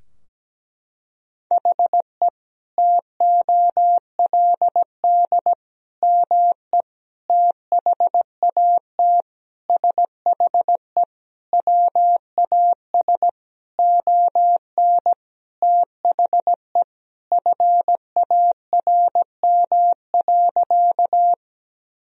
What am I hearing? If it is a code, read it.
Morse code, heard as a keyed tone, about 17 wpm, 700 Hz. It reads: HE TOLD ME THAT SHE WAS ON THE FARM.
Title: Sentences from Top 1000 Words - Review Part 2 - 17wpm